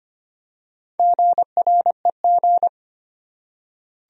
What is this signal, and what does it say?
Morse code, heard as a keyed tone, about 25 wpm, 700 Hz. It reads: GREG
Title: USA Common Names - Max 5 Letters - 25wpm